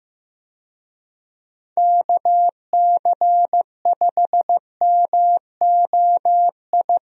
Morse code, heard as a keyed tone, about 15 wpm, 700 Hz. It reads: KC5MOI
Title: USA Call Signs - 6 Characters - Repeated 3x - 15wpm